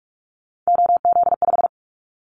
Morse code, heard as a keyed tone, about 44 wpm, 700 Hz. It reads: OZ5
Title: Callsign Prefixes for Top 50 DXCC - 45wpm